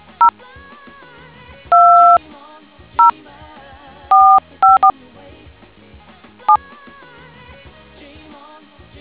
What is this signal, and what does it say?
DTMF, heard as telephone touch tones, tones of about 130 ms. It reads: 0 2 0 454 7